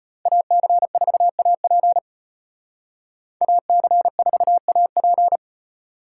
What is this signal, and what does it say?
Morse code, heard as a keyed tone, about 33 wpm, 700 Hz. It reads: AC4AP AC4AP